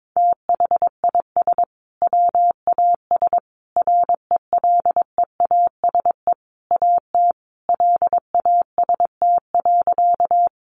Morse code, heard as a keyed tone, about 22 wpm, 700 Hz. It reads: THIS WAS RELEASE AT LAST.
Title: Sentences from Top 1100 Words 22wpm